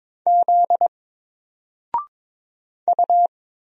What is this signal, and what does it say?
Morse code, heard as a keyed tone, about 22 wpm, 700 Hz. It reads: Z U
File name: Single-Letter-Number-22wpm